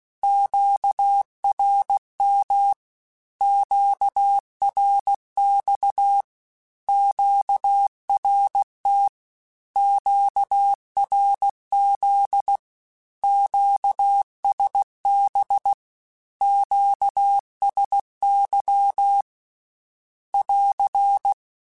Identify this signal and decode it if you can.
Morse code, heard as a keyed tone, about 16 wpm, 800 Hz. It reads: QRM QRX QRT QRZ QSB QSY <AR>
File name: Q Calls / Q Calls-mp3s / Q Calls Once @15